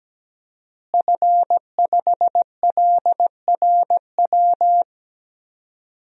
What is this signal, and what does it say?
Morse code, heard as a keyed tone, about 17 wpm, 700 Hz. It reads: F5LRW
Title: DXCC-100 Call Signs - 5 Characters - Repeated 2x - 17wpm